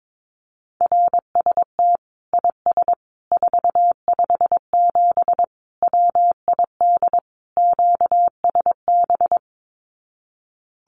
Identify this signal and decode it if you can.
Morse code, heard as a keyed tone, about 22 wpm, 700 Hz. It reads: RST IS 457 WID QSB
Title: Common QSO Phrases 22wpm